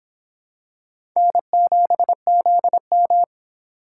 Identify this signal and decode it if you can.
Morse code, heard as a keyed tone, about 25 wpm, 700 Hz. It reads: N7ZM